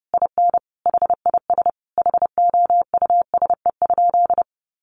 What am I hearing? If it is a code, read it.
Morse code, heard as a keyed tone, about 30 wpm, 700 Hz. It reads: IN HIS HOUSE?
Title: Sentences from Top 1200 Words 30wpm